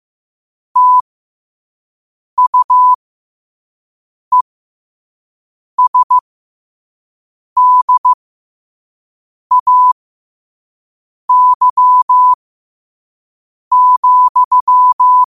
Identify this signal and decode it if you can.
Morse code, heard as a keyed tone, about 15 wpm, spaced as for 5 wpm, 1000 Hz. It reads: TUESDAY,
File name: qotdfiles / 5wpm